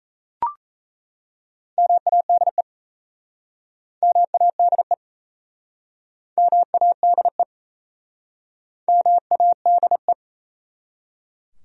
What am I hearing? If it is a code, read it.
Morse code, heard as a keyed tone, about 37 wpm, 700 Hz. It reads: MADE MADE MADE MADE